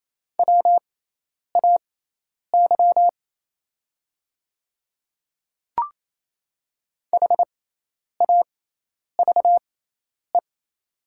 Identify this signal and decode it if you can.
Morse code, heard as a keyed tone, about 28 wpm, spaced as for 10 wpm, 700 Hz. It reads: WAY HAVE
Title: 6x Character Spacing - Top 200 Words - 28wpm